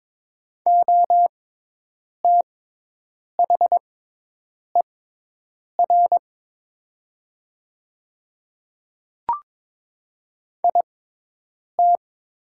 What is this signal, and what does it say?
Morse code, heard as a keyed tone, about 22 wpm, spaced as for 8 wpm, 700 Hz. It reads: OTHER IT
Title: 6x Character Spacing - Top 200 Words - 22wpm